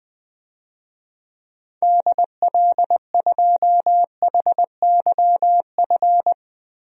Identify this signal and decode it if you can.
Morse code, heard as a keyed tone, about 20 wpm, 700 Hz. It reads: DL2HYF
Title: DXCC-100 Call Signs - 6 Characters - Repeated 2x - 20wpm